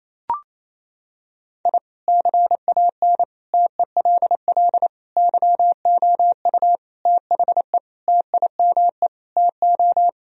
Morse code, heard as a keyed tone, about 27 wpm, 700 Hz. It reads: I CAN TELL YOU THE TIME TO
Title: Sentences from Top 1000 Words - Review Part 1 - 28wpm